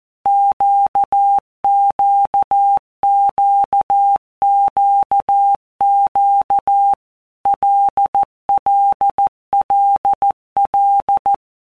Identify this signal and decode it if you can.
Morse code, heard as a keyed tone, about 14 wpm, 800 Hz. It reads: QQQQQ LLLL